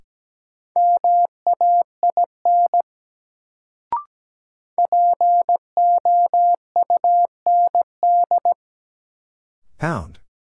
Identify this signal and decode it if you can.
Morse code, heard as a keyed tone, about 17 wpm, 700 Hz. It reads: MAIN POUND
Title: Sentences from Top 1000 Words - Review Part 2 - New Word Drill - 17wpm